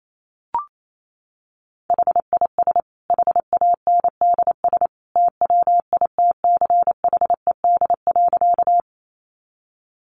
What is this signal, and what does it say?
Morse code, heard as a keyed tone, about 28 wpm, 700 Hz. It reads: HIS HANDS TWITCHED.
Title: A Sound Of Thunder 28wpm